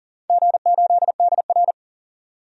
Morse code, heard as a keyed tone, about 40 wpm, 700 Hz. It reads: G8DR
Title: DXCC-100 Call Signs - 4 Characters - 40wpm